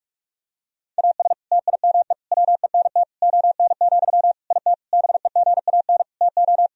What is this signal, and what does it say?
Morse code, heard as a keyed tone, about 46 wpm, 700 Hz. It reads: AS TIME WENT ON, IT BEGAN TO
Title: Sentences from Top 1000 Words - Review Part 1 - 45wpm